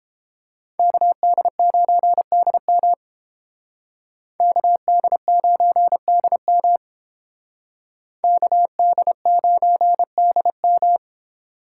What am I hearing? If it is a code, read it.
Morse code, heard as a keyed tone, about 30 wpm, 700 Hz. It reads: KD9DM KD9DM KD9DM